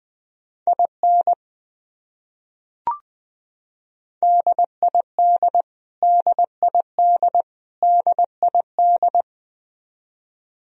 Morse code, heard as a keyed tone, about 20 wpm, 700 Hz. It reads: IN DID DID DID
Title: Top 100 Words Repeated 20wpm